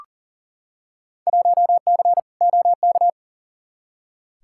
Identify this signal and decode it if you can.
Morse code, heard as a keyed tone, about 40 wpm, 700 Hz. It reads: 1C OK
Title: Field Day Exchanges Encore - 40wpm